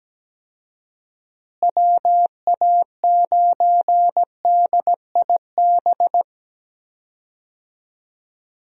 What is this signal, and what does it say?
Morse code, heard as a keyed tone, about 17 wpm, 700 Hz. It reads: WA9DIB
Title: USA Call Signs - 6 Characters - Repeated 2x - 17wpm